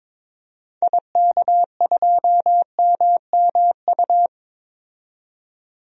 Morse code, heard as a keyed tone, about 22 wpm, 700 Hz. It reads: IK2MMU
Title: DXCC-100 Call Signs - 6 Characters - 22wpm